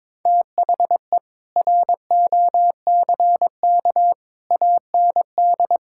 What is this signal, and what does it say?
Morse code, heard as a keyed tone, about 22 wpm, 700 Hz. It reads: THE ROCK AND
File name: practice / 22wpm / Sets of 7 Words - 22wpm